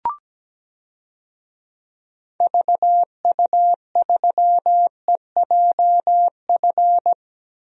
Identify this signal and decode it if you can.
Morse code, heard as a keyed tone, about 17 wpm, 700 Hz. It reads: VU3EJF
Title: DXCC-100 Call Signs - 6 Characters - Repeated 2x - 17wpm